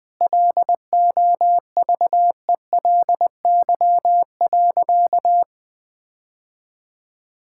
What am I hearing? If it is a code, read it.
Morse code, heard as a keyed tone, about 20 wpm, 700 Hz. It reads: LOVELY.
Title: Sentences with 3-Word Phrases - 20wpm